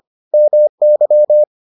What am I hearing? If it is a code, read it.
Morse code, heard as a keyed tone, about 25 wpm, 600 Hz. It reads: MY